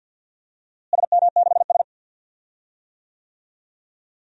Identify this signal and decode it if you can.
Morse code, heard as a keyed tone, about 50 wpm, 700 Hz. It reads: SM6S
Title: DXCC-100 Call Signs - 4 Characters - Repeated 2x - 50wpm